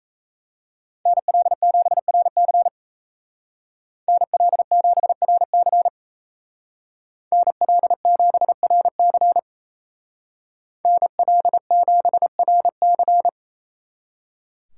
Morse code, heard as a keyed tone, about 33 wpm, 700 Hz. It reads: NL7RC NL7RC NL7RC NL7RC
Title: Speed Racing - USA 2x2 Call Signs 28wpm